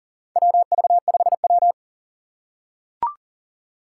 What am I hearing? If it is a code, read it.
Morse code, heard as a keyed tone, about 40 wpm, 700 Hz. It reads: WV5W
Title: USA Call Signs - 4 Characters - 2x Character Spacing - 40wpm